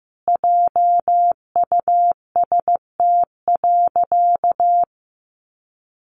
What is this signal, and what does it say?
Morse code, heard as a keyed tone, about 15 wpm, 700 Hz. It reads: JUST.
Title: Sentences from Top 1400 Words 15wpm